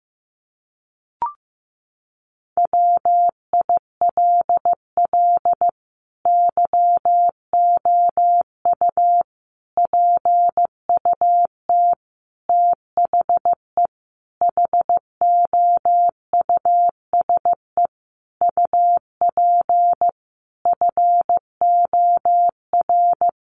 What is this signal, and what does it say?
Morse code, heard as a keyed tone, about 14 wpm, 700 Hz. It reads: WILL YOU PUT THE HOUSE UP FOR
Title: Sentences from Top 1400 Words 15wpm